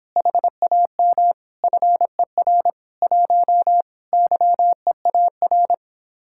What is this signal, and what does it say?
Morse code, heard as a keyed tone, about 29 wpm, 700 Hz. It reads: HAM FER 1 YEAR